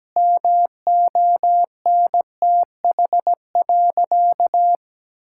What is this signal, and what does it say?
Morse code, heard as a keyed tone, about 17 wpm, 700 Hz. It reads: MONTH.
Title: Sentences from Top 1000 Words - Review Part 5 - 17wpm